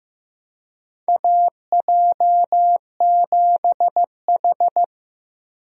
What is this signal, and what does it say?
Morse code, heard as a keyed tone, about 15 wpm, 700 Hz. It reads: AJ7H
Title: USA Call Signs - 4 Characters - Repeated 2x - 15wpm